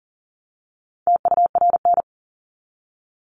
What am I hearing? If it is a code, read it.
Morse code, heard as a keyed tone, about 40 wpm, 700 Hz. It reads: TURN
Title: Top 200 Words 40wpm